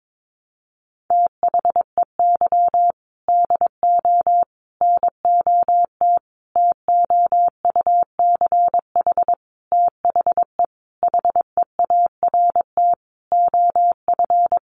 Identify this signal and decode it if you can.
Morse code, heard as a keyed tone, about 22 wpm, 700 Hz. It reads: THEY DO NOT TOUCH THE HEART OF
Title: Sentences from Top 1200 Words 22wpm